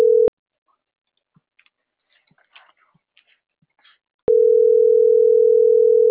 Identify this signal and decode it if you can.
Ringback tone (North America), 440 + 480 Hz.